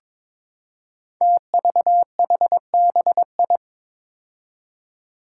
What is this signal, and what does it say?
Morse code, heard as a keyed tone, about 22 wpm, 700 Hz. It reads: TVHBI
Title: Five Challenging Letter-Number Combinations 22wpm